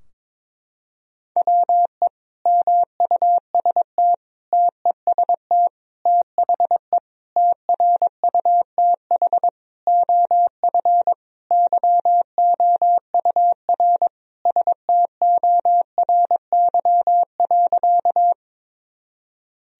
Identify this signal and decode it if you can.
Morse code, heard as a keyed tone, about 22 wpm, 700 Hz. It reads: WE MUST TEST THE TRUTH OF YOUR STORY.
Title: Sentences from Top 1000 Words - Review Part 3 - 22wpm